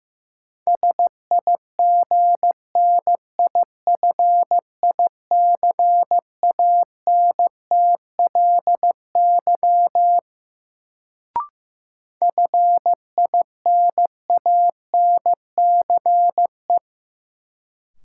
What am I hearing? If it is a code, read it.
Morse code, heard as a keyed tone, about 15 wpm, 700 Hz. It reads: SIGNIFICANTLY FINANCE
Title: Sentences from Top 1500 Words - New Word Drill - 15wpm